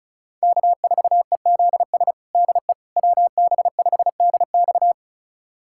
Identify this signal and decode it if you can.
Morse code, heard as a keyed tone, about 36 wpm, 700 Hz. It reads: K4EZS DE WB5DX
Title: USA Callsign DE Callsign - 5 Characters - 35wpm